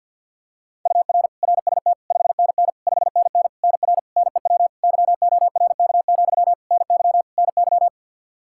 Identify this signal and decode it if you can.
Morse code, heard as a keyed tone, about 51 wpm, 700 Hz. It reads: UR RST 5NN 5NN NR NEW YORK, NY NY
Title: Practice QSO - Exchange 1 - Beginning - 50wpm